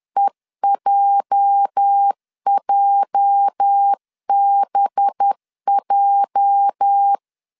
Morse code, heard as a keyed tone, about 10 wpm, 800 Hz. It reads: EJJBJ
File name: LEZIONI / 8 YJ45 / 8 YJ 10wpm 2651